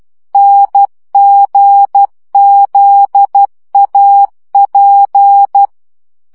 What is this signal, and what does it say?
Morse code, heard as a keyed tone, about 12 wpm, 800 Hz. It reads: NGZAP
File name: morze vizsga 2011-3 10